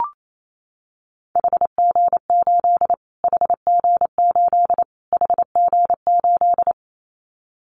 Morse code, heard as a keyed tone, about 28 wpm, 700 Hz. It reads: HG8 HG8 HG8
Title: Callsign Prefixes for Top 25 DXCC - Repeated 3x - 28wpm